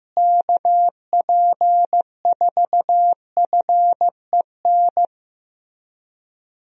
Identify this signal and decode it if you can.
Morse code, heard as a keyed tone, about 15 wpm, 700 Hz. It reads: KP4FEN
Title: DXCC-100 Call Signs - 6 Characters - Repeated 2x - 15wpm